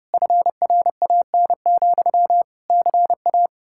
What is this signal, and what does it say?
Morse code, heard as a keyed tone, about 39 wpm, 700 Hz. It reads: FRAN, CA